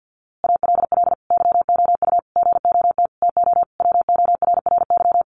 Morse code, heard as a keyed tone, about 50 wpm, 700 Hz. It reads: ALL YOU GOT TO WORRY
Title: A Sound Of Thunder 50wpm